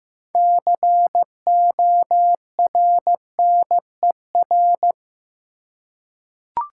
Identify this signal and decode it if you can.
Morse code, heard as a keyed tone, about 16 wpm, 700 Hz. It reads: CORNER
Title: Six Letter Words - 15wpm